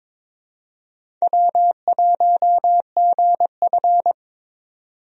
Morse code, heard as a keyed tone, about 22 wpm, 700 Hz. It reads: W1GF
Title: USA Call Signs - 4 Characters - 22wpm